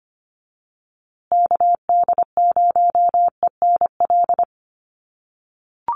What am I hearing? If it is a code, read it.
Morse code, heard as a keyed tone, about 25 wpm, 700 Hz. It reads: KD0ENL
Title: USA Call Signs Repeated 3x 25wpm